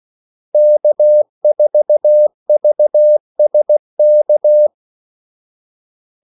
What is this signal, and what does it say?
Morse code, heard as a keyed tone, about 16 wpm, 600 Hz. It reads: K4VSK
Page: Kaldesignaler 16 wpm